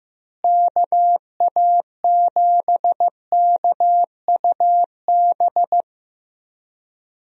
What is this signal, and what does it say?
Morse code, heard as a keyed tone, about 15 wpm, 700 Hz. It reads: KA7KUB
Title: USA Call Signs - 6 Characters - Repeated 2x - 15wpm